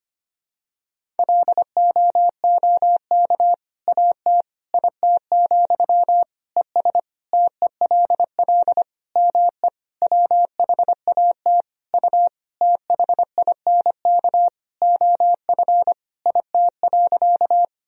Morse code, heard as a keyed tone, about 25 wpm, 700 Hz. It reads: LOOK AT IT, ES TELL ME WHAT U THINK OF IT.
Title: Sentences from Top 200 Words and Abbreviations 25wpm